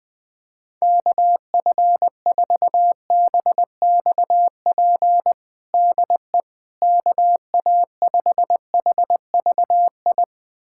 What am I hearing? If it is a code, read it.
Morse code, heard as a keyed tone, about 20 wpm, 700 Hz. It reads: KF4BXP DE KA5HVI
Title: USA Callsign DE Callsign - 6 Characters - 20wpm